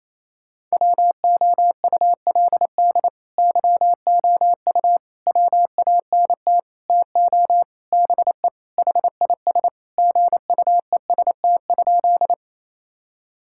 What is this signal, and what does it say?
Morse code, heard as a keyed tone, about 28 wpm, 700 Hz. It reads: WOULD YOU WANT TO BE HIS GUEST?
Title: Sentences from Top 1600 Words 28wpm